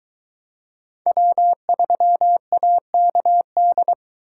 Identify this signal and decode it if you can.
Morse code, heard as a keyed tone, about 22 wpm, 700 Hz. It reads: W3AKD